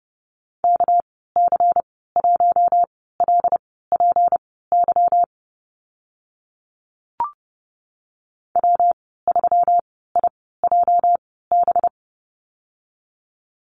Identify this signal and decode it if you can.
Morse code, heard as a keyed tone, about 30 wpm, spaced as for 17 wpm, 700 Hz. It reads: KC1LPY W3IJB
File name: Lesson 37g - Letter x - Call Signs 30wpm